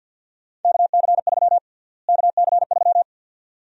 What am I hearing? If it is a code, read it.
Morse code, heard as a keyed tone, about 52 wpm, 700 Hz. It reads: KC3 KC3
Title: USA Call Sign Prefixes - Repeated 2x - 50wpm